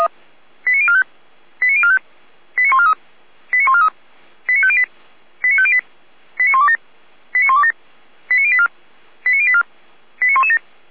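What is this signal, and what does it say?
Five-tone selective call (ZVEI), heard as quick runs of five tones, tones of about 70 ms. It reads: 89045 89045 89143 89143 89508 89508 89127 89127 89084 89084 89117
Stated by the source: ZVEI five-tone selective call received by SDR.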